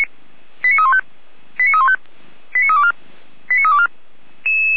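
Five-tone selective call (ZVEI), heard as quick runs of five tones, tones of about 70 ms. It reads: 89315 89315 89324 89324
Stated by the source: ZVEI five-tone call, address 89376, received on an SDR.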